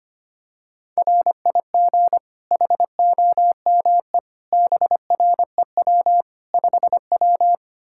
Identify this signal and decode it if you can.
Morse code, heard as a keyed tone, about 25 wpm, 700 Hz. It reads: RIG HOME BREW 5W
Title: K1USN SST QSO Practice 25wpm